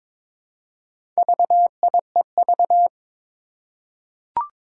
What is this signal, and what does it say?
Morse code, heard as a keyed tone, about 22 wpm, 700 Hz. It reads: VIEV
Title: Four Challenging Letter-Number Combinations 22wpm